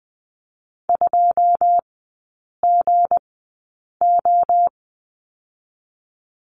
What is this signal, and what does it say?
Morse code, heard as a keyed tone, about 20 wpm, spaced as for 8 wpm, 700 Hz. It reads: 2GO
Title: 198 Lesson 32e - Letter k - Three Characters 20wpm